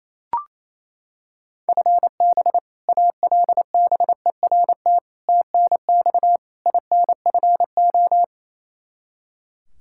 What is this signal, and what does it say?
Morse code, heard as a keyed tone, about 28 wpm, 700 Hz. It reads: FB ALBERT TNX INFO
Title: Practice QSO - Exchange 3 - More Info - 28wpm